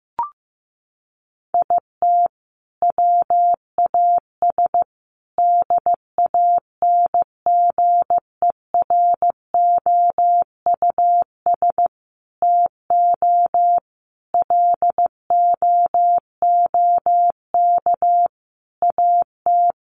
Morse code, heard as a keyed tone, about 15 wpm, 700 Hz. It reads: IT WAS DANGEROUS TO LOOK AT
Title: Sentences from Top 1300 Words 15wpm